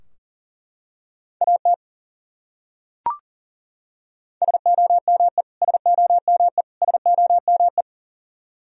Morse code, heard as a keyed tone, about 40 wpm, 700 Hz. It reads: AT SOME SOME SOME
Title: Top 100 Words Repeated 40wpm